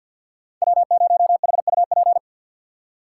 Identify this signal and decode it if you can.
Morse code, heard as a keyed tone, about 47 wpm, 700 Hz. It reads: W0HUP
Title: USA Call Signs - 4x Character Spacing - 50wpm